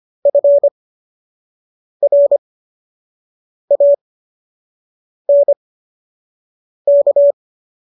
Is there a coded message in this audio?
Yes: Morse code from 0.3 to 7.3 s, heard as a keyed tone, about 25 wpm, spaced as for 6 wpm, 570 Hz. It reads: FRANK